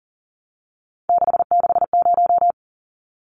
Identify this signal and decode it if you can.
Morse code, heard as a keyed tone, about 40 wpm, 700 Hz. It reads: 660
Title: Numbers - 3-Digit 40wpm